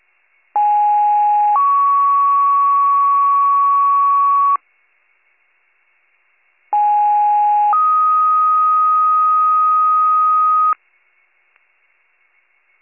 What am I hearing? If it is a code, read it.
Two-tone page, heard as a one-second tone, then a three-second tone, about 830 then 1120 Hz, about 830 then 1250 Hz. It is heard from 0.6 to 10.7 s.